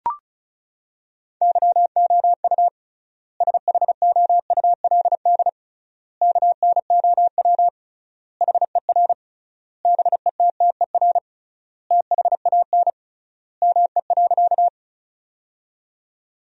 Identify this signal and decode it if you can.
Morse code, heard as a keyed tone, about 36 wpm, 700 Hz. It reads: YOU SHOULD KNOW HER BETTER THAN ME.
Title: 3x Word-Spacing - Sentences from Top 400 Words - 35wpm